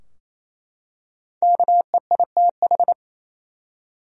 Morse code, heard as a keyed tone, about 28 wpm, 700 Hz. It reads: KEITH
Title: USA Common Names - 28wpm